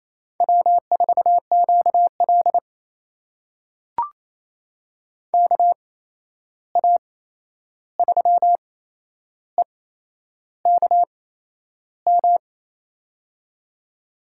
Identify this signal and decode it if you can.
Morse code, heard as a keyed tone, about 27 wpm, spaced as for 10 wpm, 700 Hz. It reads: W4QL KA3EKM
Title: USA Call Signs - 8x Character Spacing - 28wpm